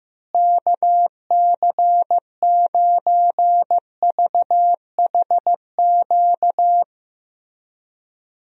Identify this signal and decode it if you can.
Morse code, heard as a keyed tone, about 15 wpm, 700 Hz. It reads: KC9VHQ
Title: USA Call Signs - 6 Characters - Repeated 2x - 15wpm